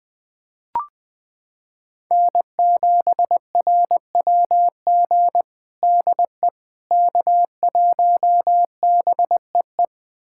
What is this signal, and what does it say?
Morse code, heard as a keyed tone, about 19 wpm, 700 Hz. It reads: N7RWG DE K1BEE
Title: USA Callsign DE Callsign - 5 Characters - 20wpm